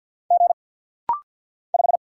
Morse code, heard as a keyed tone, about 50 wpm, 700 Hz. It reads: G 5
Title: Single Letter-Number - Rapid-Fire 50wpm